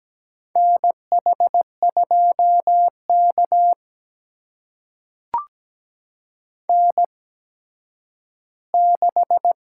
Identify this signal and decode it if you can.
Morse code, heard as a keyed tone, about 17 wpm, spaced as for 10 wpm, 700 Hz. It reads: NH2K N6
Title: USA Call Signs - 4 Characters - 8x Character Spacing - 17wpm